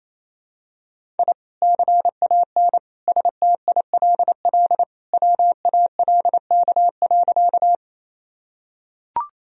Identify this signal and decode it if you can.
Morse code, heard as a keyed tone, about 28 wpm, 700 Hz. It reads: I CAN STILL WALK.
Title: Sentences from Top 1000 Words - Review Part 1 - 28wpm